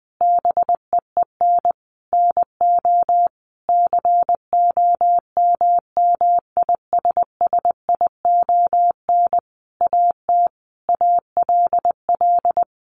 Morse code, heard as a keyed tone, about 20 wpm, 700 Hz. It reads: BEEN NO COMMISSION AT ALL
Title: Sentences from Top 1400 Words 20wpm